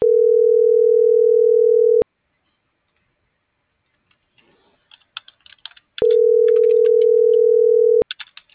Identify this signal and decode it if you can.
Ringback tone (North America), 440 + 480 Hz.